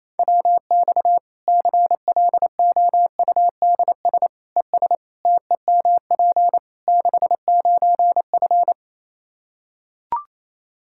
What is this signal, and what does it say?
Morse code, heard as a keyed tone, about 28 wpm, 700 Hz. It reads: WX CLOUDS ES TEMP 69F
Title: K1USN SST QSO Practice 28wpm